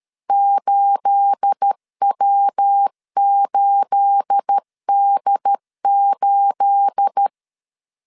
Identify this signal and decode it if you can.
Morse code, heard as a keyed tone, about 13 wpm, 800 Hz. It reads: 8W8D8
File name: LEZIONI / 9 28 12wpm 2351